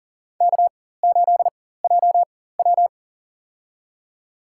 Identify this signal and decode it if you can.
Morse code, heard as a keyed tone, about 40 wpm, spaced as for 19 wpm, 700 Hz. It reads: K8JW
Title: USA Call Signs - 4 Characters - 4x Character Spacing - 40wpm